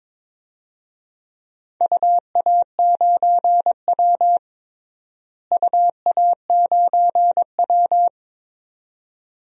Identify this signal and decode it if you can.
Morse code, heard as a keyed tone, about 22 wpm, 700 Hz. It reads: UA9W UA9W
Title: DXCC-100 Call Signs - 4 Characters - Repeated 2x - 22wpm